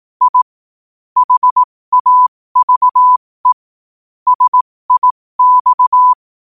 Morse code, heard as a keyed tone, about 18 wpm, spaced as for 15 wpm, 1000 Hz. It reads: I HAVE SIX
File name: qotdfiles / 15wpm